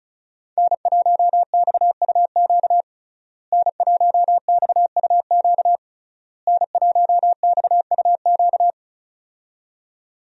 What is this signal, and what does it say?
Morse code, heard as a keyed tone, about 34 wpm, 700 Hz. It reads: N1XUQ N1XUQ N1XUQ
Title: USA Call Signs - 5 Characters - Repeated 3x - 35wpm